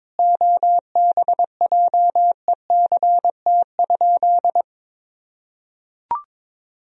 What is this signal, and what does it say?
Morse code, heard as a keyed tone, about 22 wpm, 700 Hz. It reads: OBJECT?
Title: Sentences from Top 500 Words and Abbreviations 22wpm